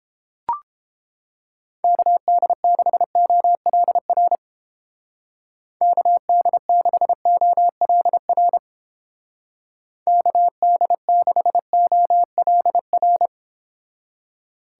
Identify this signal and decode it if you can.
Morse code, heard as a keyed tone, about 30 wpm, 700 Hz. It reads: KD6OLR KD6OLR KD6OLR